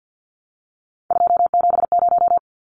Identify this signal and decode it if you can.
Morse code, heard as a keyed tone, about 50 wpm, 700 Hz. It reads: 270
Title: Numbers - 3-Digit 50wpm